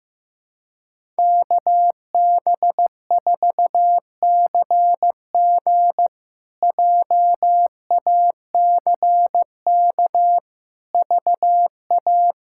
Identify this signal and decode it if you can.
Morse code, heard as a keyed tone, about 15 wpm, 700 Hz. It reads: KB4CG JACK VA
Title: K1USN SST QSO Practice - 15wpm